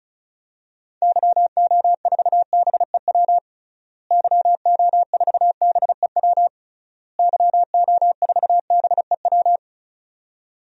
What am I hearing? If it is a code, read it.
Morse code, heard as a keyed tone, about 34 wpm, 700 Hz. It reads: YO4BEW YO4BEW YO4BEW
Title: DXCC-100 Call Signs - 6 Characters - Repeated 3x - 35wpm